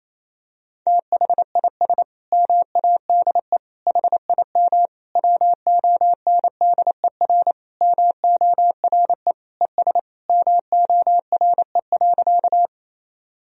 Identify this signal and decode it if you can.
Morse code, heard as a keyed tone, about 28 wpm, 700 Hz. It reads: THIS MADE HIM WONDER MORE ES MORE.
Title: Sentences from Top 500 Words and Abbreviations 28wpm